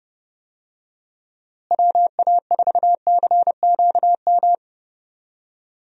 Morse code, heard as a keyed tone, about 31 wpm, 700 Hz. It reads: WA4CQM
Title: USA Call Signs - 6 Characters - Repeated 2x - 30wpm